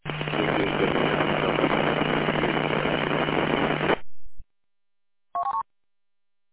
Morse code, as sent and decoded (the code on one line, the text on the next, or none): none